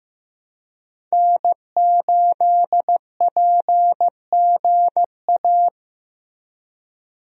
Morse code, heard as a keyed tone, about 15 wpm, 700 Hz. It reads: N8PGA
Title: USA Call Signs - 5 Characters - Repeated 2x - 15wpm